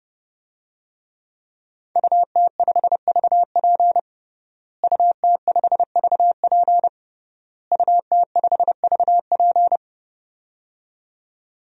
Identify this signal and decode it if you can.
Morse code, heard as a keyed tone, about 30 wpm, 700 Hz. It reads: UT5VP UT5VP UT5VP
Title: DXCC-100 Call Signs - 5 Characters - Repeated 3x - 30wpm